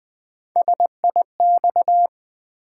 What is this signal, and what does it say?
Morse code, heard as a keyed tone, about 20 wpm, 700 Hz. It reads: SIX
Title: Build A Word - 1 Letter At A Time - 3-Letter Words - 20wpm